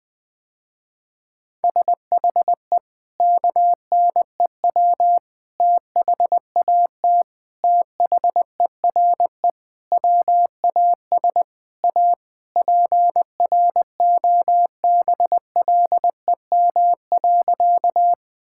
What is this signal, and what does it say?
Morse code, heard as a keyed tone, about 20 wpm, 700 Hz. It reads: SHE KNEW THAT THERE WAS A PROBLEM.
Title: Sentences with 4-Word Phrases - Spoken Beginning - 20wpm